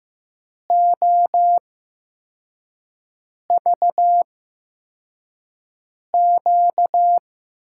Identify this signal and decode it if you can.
Morse code, heard as a keyed tone, about 15 wpm, spaced as for 4 wpm, 700 Hz. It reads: OVQ
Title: USA Call Signs - 6 Characters - 8x Character Spacing - 15wpm